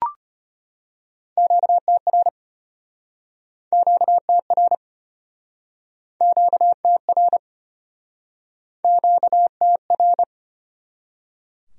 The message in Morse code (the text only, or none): QTR QTR QTR QTR